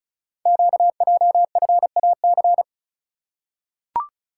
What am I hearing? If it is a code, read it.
Morse code, heard as a keyed tone, about 34 wpm, 690 Hz. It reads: QJFAC
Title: Code Groups - 5 Letters - 35wpm